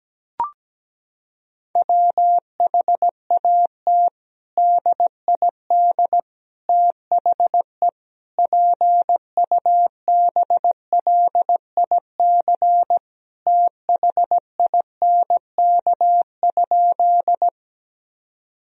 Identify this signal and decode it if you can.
Morse code, heard as a keyed tone, about 17 wpm, 700 Hz. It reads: WHAT DID THE PUBLIC THINK?
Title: Sentences from Top 1000 Words - Review Part 3 - 17wpm